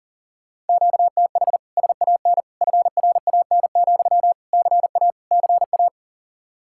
Morse code, heard as a keyed tone, about 51 wpm, 700 Hz. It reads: QTH SAN FRAN, CA CA